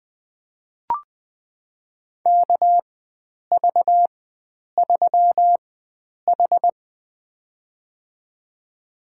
Morse code, heard as a keyed tone, about 20 wpm, spaced as for 9 wpm, 700 Hz. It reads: KV3H
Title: USA Call Signs - 4x Character Spacing - 20wpm